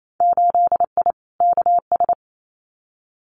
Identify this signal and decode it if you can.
Morse code, heard as a keyed tone, about 28 wpm, 700 Hz. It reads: 8I KS
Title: Winter Field Day Exchanges Encore 28wpm